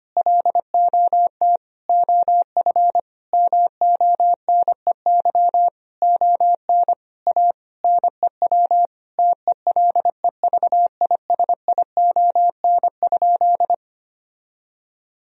Morse code, heard as a keyed tone, about 25 wpm, 700 Hz. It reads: LOT OF MONEY ON A NEW TELEVISION?
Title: Sentences from Top 1000 Words - Review Part 4 - 25wpm